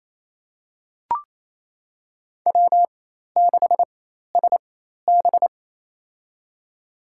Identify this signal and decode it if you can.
Morse code, heard as a keyed tone, about 28 wpm, spaced as for 13 wpm, 700 Hz. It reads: W6SB
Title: USA Call Signs - 4 Characters - 4x Character Spacing - 28wpm